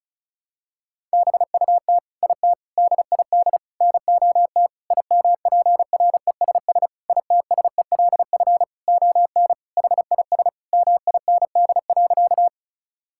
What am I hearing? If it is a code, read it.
Morse code, heard as a keyed tone, about 34 wpm, 700 Hz. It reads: BUT IT DID NOT IMPRESS ITSELF ON HIS MIND.
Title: Sentences from Top 1600 Words 35wpm